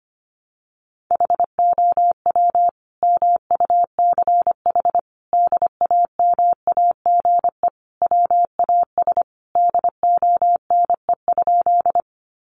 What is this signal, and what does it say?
Morse code, heard as a keyed tone, about 25 wpm, 700 Hz. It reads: HOW MUCH DAMAGE WAS DONE?
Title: Sentences from Top 1100 Words 25wpm